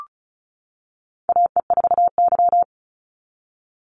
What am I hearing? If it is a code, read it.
Morse code, heard as a keyed tone, about 34 wpm, 700 Hz. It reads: AE4Y
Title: NA SOTA Chaser Callsigns 35wpm